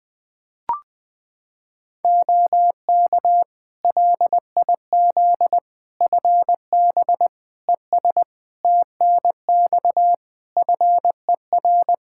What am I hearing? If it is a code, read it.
Morse code, heard as a keyed tone, about 20 wpm, 700 Hz. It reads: OK LIZ FB ES TNX FER
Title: K1USN SST QSO Practice 20wpm